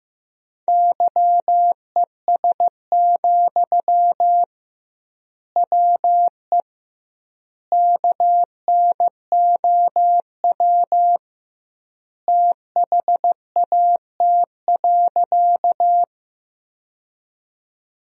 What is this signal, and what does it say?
Morse code, heard as a keyed tone, about 15 wpm, 700 Hz. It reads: YES, WE KNOW THAT.
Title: 2x Word-Spacing - Sentences from Top 500 Words - 15wpm